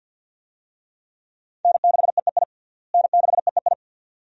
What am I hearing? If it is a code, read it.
Morse code, heard as a keyed tone, about 47 wpm, 700 Hz. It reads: N6EEI N6EEI
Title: USA Call Signs - 5 Characters - Repeated 2x - 50wpm